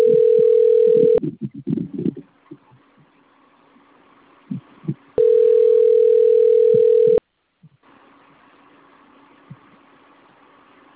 Ringback tone (North America), 440 + 480 Hz.